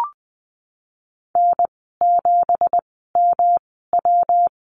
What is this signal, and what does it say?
Morse code, heard as a keyed tone, about 20 wpm, spaced as for 15 wpm, 700 Hz. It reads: N7MW
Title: 167 Lesson 27g - Number 7 - Call Signs 20wpm